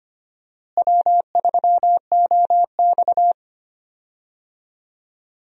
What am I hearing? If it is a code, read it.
Morse code, heard as a keyed tone, about 25 wpm, 700 Hz. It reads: W3OX
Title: USA Call Signs - 4 Characters - Repeated 2x - 25wpm